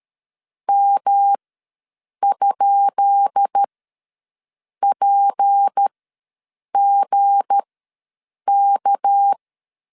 Morse code, heard as a keyed tone, about 13 wpm, spaced as for 7 wpm, 800 Hz. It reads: M?PGK
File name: LEZIONI / 10 SIGNS / 10B INT ws 12wpm 3527